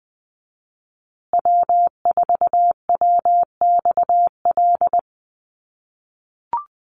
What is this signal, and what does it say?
Morse code, heard as a keyed tone, about 20 wpm, 700 Hz. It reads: W4WXL
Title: USA Call Signs Repeated 2x 20wpm